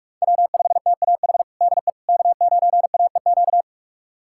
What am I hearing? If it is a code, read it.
Morse code, heard as a keyed tone, about 42 wpm, 700 Hz. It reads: W5TAH DE K9AEQ
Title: USA Callsign DE Callsign - 5 Characters - 45wpm